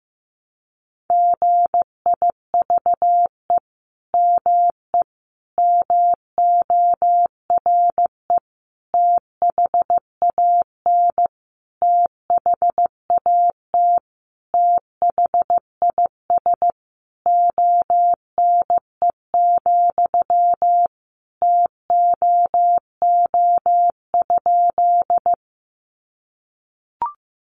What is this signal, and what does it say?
Morse code, heard as a keyed tone, about 15 wpm, 700 Hz. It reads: GIVE ME MORE THAN THAT THIS ONE, TOO?
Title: Sentences from Top 200 Words 15wpm